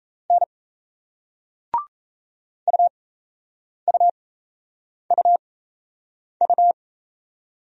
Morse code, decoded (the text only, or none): N UUUU